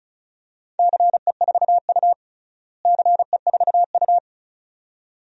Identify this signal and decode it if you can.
Morse code, heard as a keyed tone, about 34 wpm, 690 Hz. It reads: CE4U CE4U
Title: DXCC-100 Call Signs - 4 Characters - Repeated 2x - 35wpm